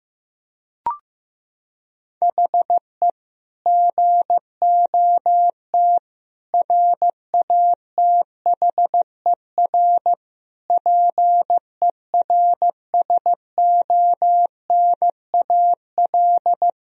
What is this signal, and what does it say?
Morse code, heard as a keyed tone, about 15 wpm, 700 Hz. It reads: HE GOT RATHER PERSONAL
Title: Sentences from Top 1000 Words - Review Part 4 - 15wpm